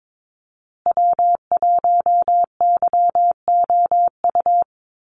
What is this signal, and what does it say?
Morse code, heard as a keyed tone, about 22 wpm, 700 Hz. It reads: W1YOU
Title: USA Call Signs Repeated 3x 22wpm